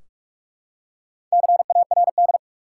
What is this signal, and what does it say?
Morse code, heard as a keyed tone, about 44 wpm, 700 Hz. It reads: CARD
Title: Four Letter Words - 45wpm